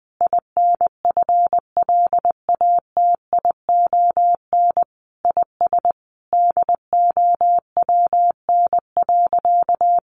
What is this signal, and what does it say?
Morse code, heard as a keyed tone, about 20 wpm, 700 Hz. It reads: INFLATION IS DOWN.
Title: Sentences from Top 1100 Words 20wpm